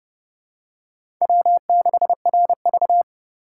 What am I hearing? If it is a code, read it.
Morse code, heard as a keyed tone, about 30 wpm, 700 Hz. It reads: W6RV
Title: USA Call Signs - 4 Characters - 6x Character Spacing - 30wpm